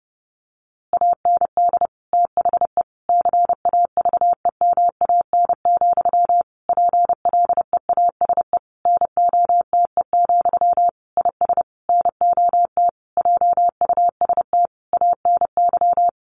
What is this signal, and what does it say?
Morse code, heard as a keyed tone, about 30 wpm, 700 Hz. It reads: AND THE CAVEMAN, PLEASE NOTE, IS NOT JUST ANY
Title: A Sound Of Thunder 30wpm